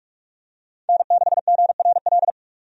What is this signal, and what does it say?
Morse code, heard as a keyed tone, about 44 wpm, 700 Hz. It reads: N6GRL
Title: USA Call Signs - 5 Characters - 45wpm